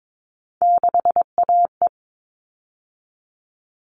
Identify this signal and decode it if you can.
Morse code, heard as a keyed tone, about 22 wpm, 700 Hz. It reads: 6AE
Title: All Cut Numbers - 3-Digit 22wpm